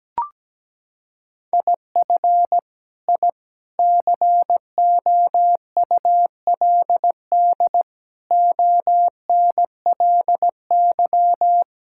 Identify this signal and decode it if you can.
Morse code, heard as a keyed tone, about 17 wpm, 700 Hz. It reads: IF I COULD ONLY
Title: Sentences from Top 1000 Words - Review Part 6 - 17wpm